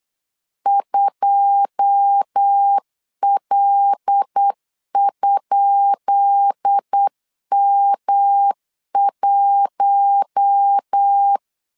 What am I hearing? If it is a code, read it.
Morse code, heard as a keyed tone, about 8 wpm, 800 Hz. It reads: 2L?M1
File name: LEZIONI / 10 SIGNS / 10B INT 8wpm 3534